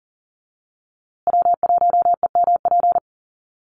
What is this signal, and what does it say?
Morse code, heard as a keyed tone, about 40 wpm, 700 Hz. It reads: W1EMP
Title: USA Call Signs Repeated 3x 40wpm